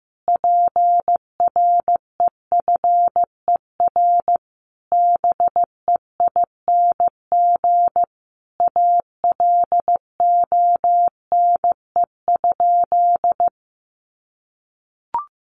Morse code, heard as a keyed tone, about 15 wpm, 700 Hz. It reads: PREFER BEING ALONE?
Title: Sentences from Top 1100 Words 15wpm